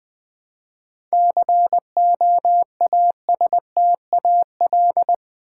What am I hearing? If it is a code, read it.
Morse code, heard as a keyed tone, about 20 wpm, 700 Hz. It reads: COASTAL
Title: 7 Letter Words - First 3 Letters Given - 20wpm